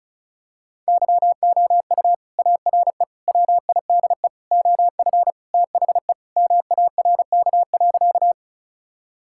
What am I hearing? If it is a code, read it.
Morse code, heard as a keyed tone, about 34 wpm, 700 Hz. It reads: YOU ARE WIDE OF THE MARK.
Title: Sentences from Top 1000 Words - Review Part 6 - 35wpm